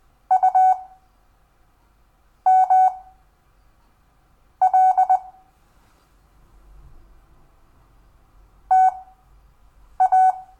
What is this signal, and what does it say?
Morse code, heard as a keyed tone, about 20 wpm, spaced as for 5 wpm, 750 Hz. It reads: UML TA
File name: cw / Lesson5 LO